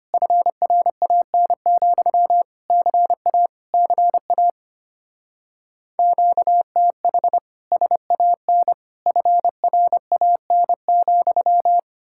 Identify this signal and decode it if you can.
Morse code, heard as a keyed tone, about 31 wpm, 700 Hz. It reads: FRAN, CA CA QTH SAN FRAN,
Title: Speed-Racing Common QSO Phrases Encore - 25wpm